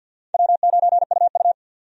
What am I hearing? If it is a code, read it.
Morse code, heard as a keyed tone, about 53 wpm, 700 Hz. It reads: W9UU
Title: USA Call Signs - 4 Characters - 50wpm